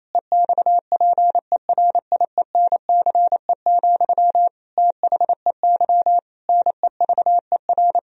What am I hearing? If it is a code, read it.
Morse code, heard as a keyed tone, about 28 wpm, 700 Hz. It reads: EXPERIENCE, THEY NEVER
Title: Sentences from Top 1000 Words - Review Part 3 - 28wpm